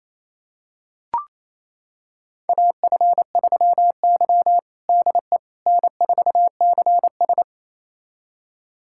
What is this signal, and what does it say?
Morse code, heard as a keyed tone, about 28 wpm, 700 Hz. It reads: AF3Y DE N4CS
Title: USA Callsign DE Callsign - 4 Characters - 28wpm